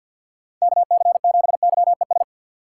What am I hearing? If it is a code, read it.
Morse code, heard as a keyed tone, about 46 wpm, 700 Hz. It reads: KC7YES
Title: USA Call Signs - 6 Characters - 4x Character Spacing - 50wpm